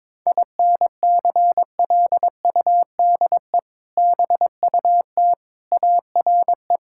Morse code, heard as a keyed tone, about 22 wpm, 700 Hz. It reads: INCLUDE BUT ARE
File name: practice / 22wpm / Sets of 6 Words - 22wpm